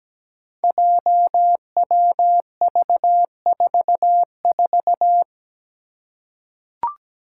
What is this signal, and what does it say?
Morse code, heard as a keyed tone, about 17 wpm, 700 Hz. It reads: JWV44
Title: Five Challenging Letter-Number Combinations 17wpm